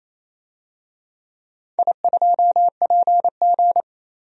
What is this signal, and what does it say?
Morse code, heard as a keyed tone, about 28 wpm, 700 Hz. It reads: I2PG
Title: DXCC-100 Call Signs - 4 Characters - 28wpm